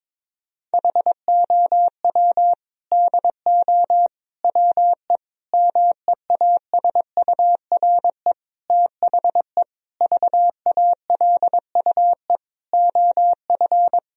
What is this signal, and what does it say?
Morse code, heard as a keyed tone, about 23 wpm, 700 Hz. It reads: HOW DO WE MEASURE THE VALUE OF
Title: Sentences from Top 1000 Words - Review Part 3 - 22wpm